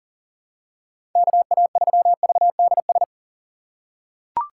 Morse code, heard as a keyed tone, about 40 wpm, 700 Hz. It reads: KA3VDS
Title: USA Call Signs - 6 Characters - 4x Character Spacing - 40wpm